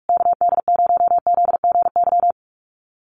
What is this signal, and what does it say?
Morse code, heard as a keyed tone, about 44 wpm, 700 Hz. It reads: KD0ZGY
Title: USA Call Signs Repeated 3x 45wpm